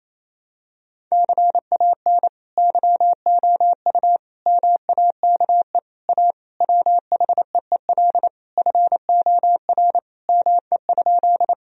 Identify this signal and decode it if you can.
Morse code, heard as a keyed tone, about 28 wpm, 700 Hz. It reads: CAN YOU MAKE A WHEEL FOR ME?
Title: Sentences from Top 1000 Words - Review Part 3 - 28wpm